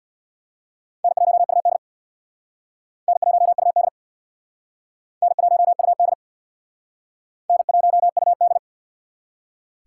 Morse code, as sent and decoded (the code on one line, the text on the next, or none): -. .---- ..- -.. / -. .---- ..- -.. / -. .---- ..- -.. / -. .---- ..- -..
N1UD N1UD N1UD N1UD